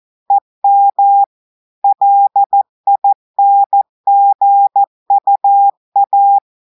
Morse code, heard as a keyed tone, about 14 wpm, 800 Hz. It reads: EM LINGUA
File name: audios / portugais / portugais texte en clair 1 vitesse 14 mots minutes